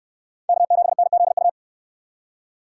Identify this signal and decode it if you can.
Morse code, heard as a keyed tone, about 69 wpm, 700 Hz. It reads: K7NZU